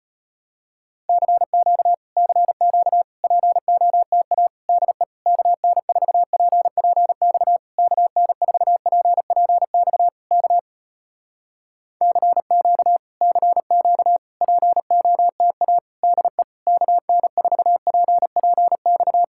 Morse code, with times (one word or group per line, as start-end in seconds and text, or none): 1.1-1.9 CQ
2.2-3.0 CQ
3.2-4.5 POTA
4.7-5.0 DE
5.3-7.6 KN4PPX
7.8-10.1 KN4PPX
10.3-10.6 K
12.0-13.0 CQ
13.2-14.2 CQ
14.4-15.8 POTA
16.0-16.4 DE
16.7-19.2 KN4PPX